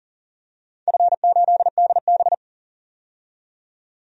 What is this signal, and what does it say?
Morse code, heard as a keyed tone, about 40 wpm, 700 Hz. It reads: F8DB
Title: DXCC-100 Call Signs - 4 Characters - 40wpm